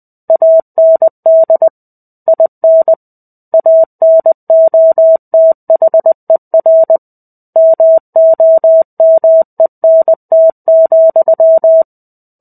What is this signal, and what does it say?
Morse code, heard as a keyed tone, about 20 wpm, 650 Hz. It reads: AND IN ANOTHER MOMENT,
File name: 05-Chapter-05 20 WPM